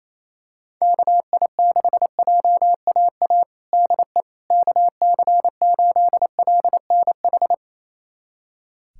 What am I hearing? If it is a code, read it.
Morse code, heard as a keyed tone, about 28 wpm, 700 Hz. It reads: KI6JAA DE KC8LNH
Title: USA Callsign DE Callsign 28wpm